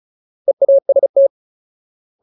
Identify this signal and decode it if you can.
Morse code, heard as a keyed tone, about 34 wpm, 550 Hz. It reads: EAST
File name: Ultimate Word List / 35WPM Ultimate